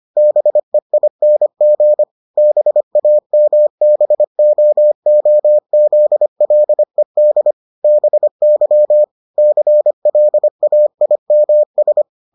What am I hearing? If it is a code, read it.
Morse code, heard as a keyed tone, about 25 wpm, 600 Hz. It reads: BEING BAMBOOZLED BY CLAIMS